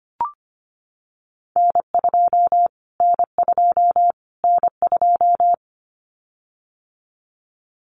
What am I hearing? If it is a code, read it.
Morse code, heard as a keyed tone, about 25 wpm, 700 Hz. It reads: N2 N2 N2
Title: Callsign Prefixes for Top 25 DXCC - Repeated 3x - 25wpm